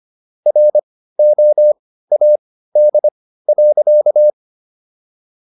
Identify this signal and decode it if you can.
Morse code, heard as a keyed tone, about 25 wpm, spaced as for 15 wpm, 600 Hz. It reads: ROAD.